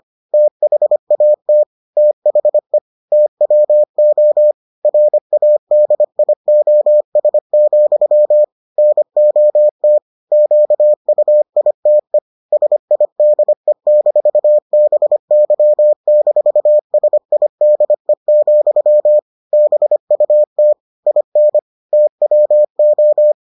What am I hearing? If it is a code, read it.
Morse code, heard as a keyed tone, about 25 wpm, 600 Hz. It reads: THAT THE TWO RADIOS, NOT QUITE SIDE-BY-SIDE, BUT IN TWO